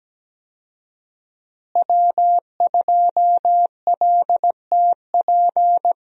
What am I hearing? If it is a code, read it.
Morse code, heard as a keyed tone, about 17 wpm, 700 Hz. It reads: W2LTP
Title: USA Call Signs - 5 Characters - Repeated 3x - 17wpm